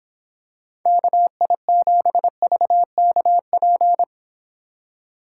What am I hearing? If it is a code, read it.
Morse code, heard as a keyed tone, about 25 wpm, 700 Hz. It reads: KI7VKP